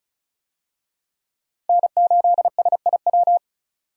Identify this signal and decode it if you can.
Morse code, heard as a keyed tone, about 36 wpm, 700 Hz. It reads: N8SIW
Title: USA Call Signs - 5 Characters - 35wpm